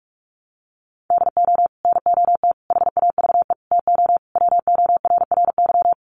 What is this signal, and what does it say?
Morse code, heard as a keyed tone, about 44 wpm, 700 Hz. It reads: DO NOT HAVE TO WORRY
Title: Sets of 5 Words 45wpm